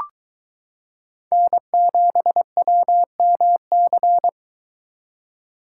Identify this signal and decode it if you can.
Morse code, heard as a keyed tone, about 23 wpm, 700 Hz. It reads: N7WMC